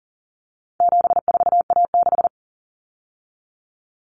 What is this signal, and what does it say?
Morse code, heard as a keyed tone, about 39 wpm, 700 Hz. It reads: 74A6
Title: Common Cut Numbers - 4-Digit 40wpm